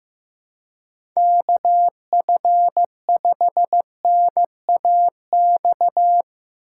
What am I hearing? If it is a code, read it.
Morse code, heard as a keyed tone, about 15 wpm, 700 Hz. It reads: KF5NAX
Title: USA Call Signs - 6 Characters - 4x Character Spacing - 15wpm